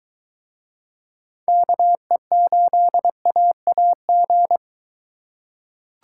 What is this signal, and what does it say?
Morse code, heard as a keyed tone, about 23 wpm, 700 Hz. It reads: KE8AAG